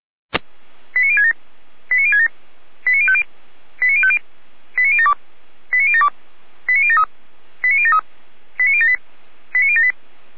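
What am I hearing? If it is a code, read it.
Five-tone selective call (ZVEI), heard as quick runs of five tones, tones of about 80 ms. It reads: 89067 89067 89050 89050 89072 89072 89073 89073 89078 89078
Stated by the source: Five-tone ZVEI selective call received on an SDR.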